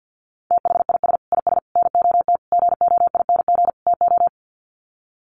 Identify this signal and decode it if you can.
Morse code, heard as a keyed tone, about 50 wpm, 700 Hz. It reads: THIS IS NOT GOING TO
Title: Sets of 5 Words Encore2 50wpm